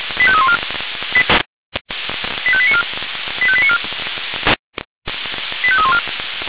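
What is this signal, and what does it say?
Five-tone selective call (ZVEI), heard as quick runs of five tones, tones of about 70 ms. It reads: 85314 85894 85894 85314
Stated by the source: ZVEI five-tone selective call received on an SDR.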